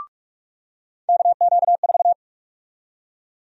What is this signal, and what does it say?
Morse code, heard as a keyed tone, about 46 wpm, 700 Hz. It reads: KQ4
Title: USA Call Sign Prefixes - 45wpm